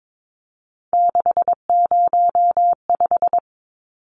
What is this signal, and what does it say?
Morse code, heard as a keyed tone, about 22 wpm, 700 Hz. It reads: 605
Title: Numbers - 3-Digit 22wpm